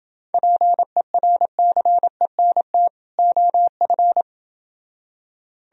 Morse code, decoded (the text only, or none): PERCENT OF